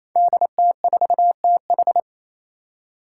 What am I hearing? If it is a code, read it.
Morse code, heard as a keyed tone, about 27 wpm, 700 Hz. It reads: DT4TH
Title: Five Challenging Letter-Number Combinations 28wpm